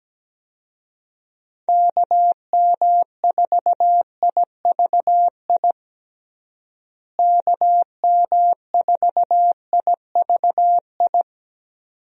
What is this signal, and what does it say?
Morse code, heard as a keyed tone, about 17 wpm, 700 Hz. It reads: KM4IVI KM4IVI
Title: USA Call Signs - 6 Characters - Repeated 2x - 17wpm